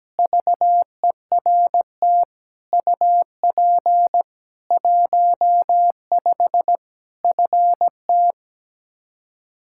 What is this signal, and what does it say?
Morse code, heard as a keyed tone, about 17 wpm, 700 Hz. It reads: VERT UP 15 FT
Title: K1USN SST QSO Practice 17wpm